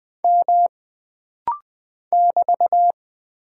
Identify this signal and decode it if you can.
Morse code, heard as a keyed tone, about 20 wpm, 700 Hz. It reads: M <BT>
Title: Single Character (Letter:Number:Punctuation:Prosign) - Rapid-Fire - 20wpm